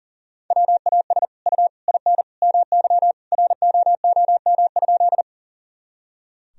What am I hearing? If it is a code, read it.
Morse code, heard as a keyed tone, about 40 wpm, 700 Hz. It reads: WAS U IN MY ROOM?
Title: Sentences from Top 300 Words and Abbreviations 40wpm